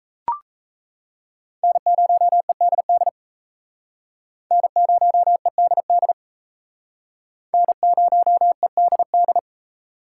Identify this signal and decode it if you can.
Morse code, heard as a keyed tone, about 36 wpm, 700 Hz. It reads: N0EDD N0EDD N0EDD